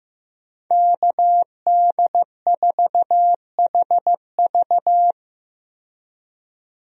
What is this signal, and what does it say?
Morse code, heard as a keyed tone, about 15 wpm, 700 Hz. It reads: KD4HV
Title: USA Call Signs - 5 Characters - 15wpm